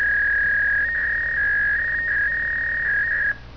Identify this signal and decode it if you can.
Fax machine handshake.